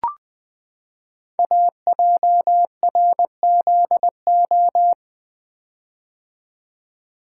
Morse code, heard as a keyed tone, about 20 wpm, 700 Hz. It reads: AJRZO
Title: Code Groups - 5 Letter:Numbers - 20wpm